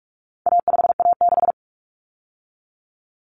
Morse code, heard as a keyed tone, about 44 wpm, 700 Hz. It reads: A5A6
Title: Common Cut Numbers - 4-Digit 45wpm